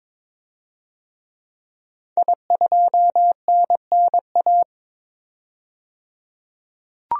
Morse code, heard as a keyed tone, about 22 wpm, 700 Hz. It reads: I2NNA
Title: DXCC-100 Call Signs - 5 Characters - 22wpm